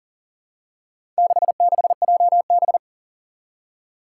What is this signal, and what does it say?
Morse code, heard as a keyed tone, about 40 wpm, 700 Hz. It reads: 66JB
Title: Four Challenging Letter-Number Combinations 40wpm